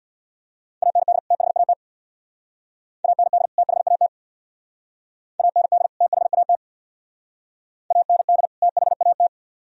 Morse code, heard as a keyed tone, about 60 wpm, 700 Hz. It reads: AND THAT AND THAT AND THAT AND THAT
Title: Speed-Racing - Sets of 2 Words 50wpm